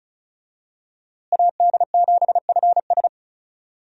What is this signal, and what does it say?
Morse code, heard as a keyed tone, about 34 wpm, 700 Hz. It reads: AD7FS
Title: USA Call Signs - 5 Characters - Repeated 2x - 35wpm